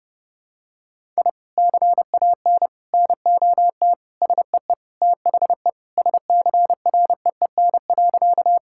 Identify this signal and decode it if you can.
Morse code, heard as a keyed tone, about 30 wpm, 700 Hz. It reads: I CAN NOT SEE THE SCREEN.
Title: Sentences from Top 1000 Words - Review Part 6 - 30wpm